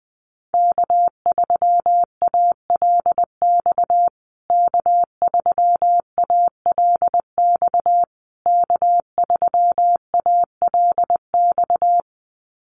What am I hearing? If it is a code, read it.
Morse code, heard as a keyed tone, about 20 wpm, 700 Hz. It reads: K3ALX K3ALX K3ALX
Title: USA Call Signs Repeated 3x 20wpm